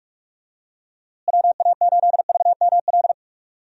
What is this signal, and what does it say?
Morse code, heard as a keyed tone, about 46 wpm, 700 Hz. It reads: WA8VML
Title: USA Call Signs - 6 Characters - Repeated 3x - 45wpm